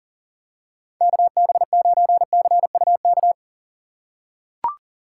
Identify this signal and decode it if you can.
Morse code, heard as a keyed tone, about 40 wpm, 700 Hz. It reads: KB9CUK
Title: USA Call Signs - 6 Characters - 6x Character Spacing - 40wpm